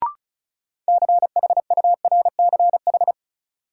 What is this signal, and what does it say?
Morse code, heard as a keyed tone, about 34 wpm, 700 Hz. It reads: CHURCH
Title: Sets of 5 Words - New Word Drill - 35wpm